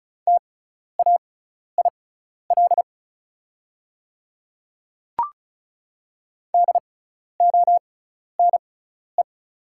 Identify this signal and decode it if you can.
Morse code, heard as a keyed tone, about 36 wpm, spaced as for 12 wpm, 690 Hz. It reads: TAIL DONE
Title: 6x Character Spacing - Top 500 Words - 35wpm